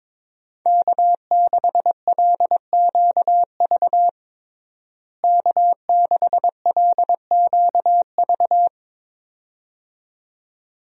Morse code, heard as a keyed tone, about 22 wpm, 700 Hz. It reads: K6LQV K6LQV
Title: USA Call Signs - 5 Characters - Repeated 2x - 22wpm